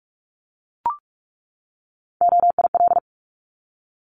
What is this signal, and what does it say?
Morse code, heard as a keyed tone, about 44 wpm, 700 Hz. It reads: OIL
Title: Top 501-750 Words 45wpm